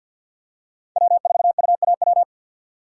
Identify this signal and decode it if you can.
Morse code, heard as a keyed tone, about 47 wpm, 700 Hz. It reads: W4UAW
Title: USA Call Signs - 5 Characters - 2x Character Spacing - 50wpm